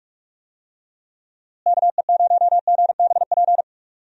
Morse code, heard as a keyed tone, about 46 wpm, 700 Hz. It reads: KE0GBP
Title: USA Call Signs - 6 Characters - Repeated 3x - 45wpm